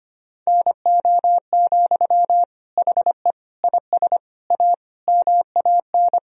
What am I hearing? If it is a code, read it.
Morse code, heard as a keyed tone, about 25 wpm, 700 Hz. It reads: NO, HE IS A MAN
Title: Sentences from Top 1000 Words - Review Part 2 - 25wpm